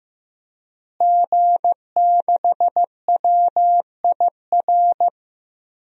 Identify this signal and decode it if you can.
Morse code, heard as a keyed tone, about 15 wpm, 700 Hz. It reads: G6WIR
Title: DXCC-100 Call Signs - 5 Characters - Repeated 3x - 15wpm